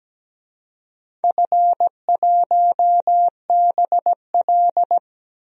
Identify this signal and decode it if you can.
Morse code, heard as a keyed tone, about 17 wpm, 700 Hz. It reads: F1BL
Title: DXCC-100 Call Signs - 4 Characters - 17wpm